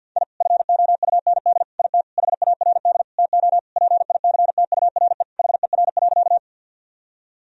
Morse code, heard as a keyed tone, about 53 wpm, 700 Hz. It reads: I FOUND IT HARD TO PICTURE HER.
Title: Sentences from Top 1000 Words - Review Part 1 - 50wpm